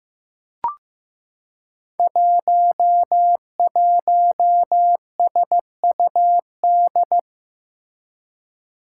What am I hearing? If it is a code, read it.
Morse code, heard as a keyed tone, about 15 wpm, 700 Hz. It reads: 11SUD
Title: Five Challenging Letter-Number Combinations 15wpm